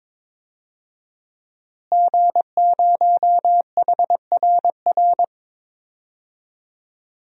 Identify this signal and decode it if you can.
Morse code, heard as a keyed tone, about 22 wpm, 700 Hz. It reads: G0HRR
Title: DXCC-100 Call Signs - 5 Characters - Repeated 3x - 22wpm